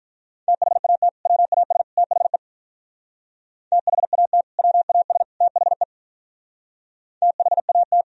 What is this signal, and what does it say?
Morse code, heard as a keyed tone, about 43 wpm, 700 Hz. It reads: THAT WAS THE THAT WAS THE THAT
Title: Speed-Racing - Sets of 3 Words 35wpm